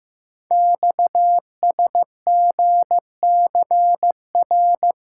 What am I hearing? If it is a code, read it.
Morse code, heard as a keyed tone, about 15 wpm, 700 Hz. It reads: XSGCR
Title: Code Groups - 5 Letters - 15wpm